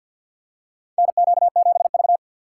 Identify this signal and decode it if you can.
Morse code, heard as a keyed tone, about 49 wpm, 700 Hz. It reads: NQ7V
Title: USA Call Signs - 4x Character Spacing - 50wpm